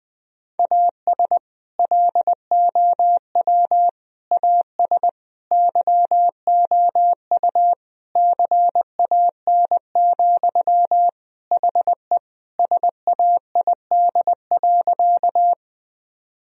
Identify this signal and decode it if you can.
Morse code, heard as a keyed tone, about 21 wpm, 700 Hz. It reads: AS LOW AS YOU CAN, HE SAID.
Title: Sentences from Top 1000 Words - Review Part 1 - 20wpm